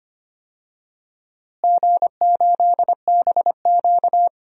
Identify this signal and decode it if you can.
Morse code, heard as a keyed tone, about 25 wpm, 700 Hz. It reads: G8BQ
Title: DXCC-100 Call Signs - 4 Characters - Repeated 3x - 25wpm